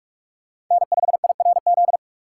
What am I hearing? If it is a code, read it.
Morse code, heard as a keyed tone, about 43 wpm, 700 Hz. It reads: N5IRZ
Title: USA Call Signs - 3x Character Spacing - 45wpm